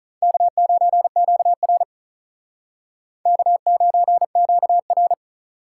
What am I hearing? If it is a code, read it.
Morse code, heard as a keyed tone, about 44 wpm, 700 Hz. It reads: K9QR K9QR